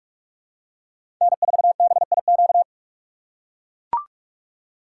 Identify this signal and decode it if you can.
Morse code, heard as a keyed tone, about 42 wpm, 700 Hz. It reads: N4BIQ
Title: USA Call Signs - 5 Characters - 4x Character Spacing - 45wpm